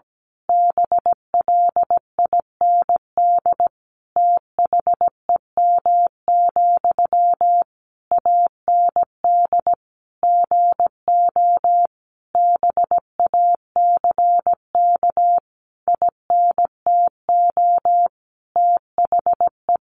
Morse code, heard as a keyed tone, about 17 wpm, 700 Hz. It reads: BLIND THEM, AND GO BACK INTO THE
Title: A Sound Of Thunder 17wpm